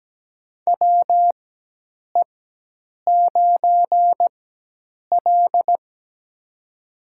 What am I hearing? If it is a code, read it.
Morse code, heard as a keyed tone, about 17 wpm, spaced as for 8 wpm, 700 Hz. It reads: WE9L
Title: USA Call Signs - 4x Character Spacing - 17wpm